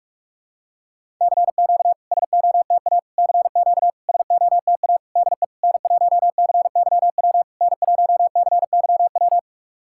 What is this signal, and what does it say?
Morse code, heard as a keyed tone, about 44 wpm, 690 Hz. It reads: CQ SOTA CQ SOTA DE N1CYW N1CYW
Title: SOTA QSO Practice 45wpm